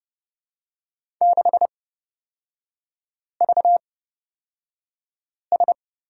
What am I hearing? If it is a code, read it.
Morse code, heard as a keyed tone, about 30 wpm, spaced as for 5 wpm, 700 Hz. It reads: 6VS
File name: Lesson 37e - Letter x - Three Characters 30wpm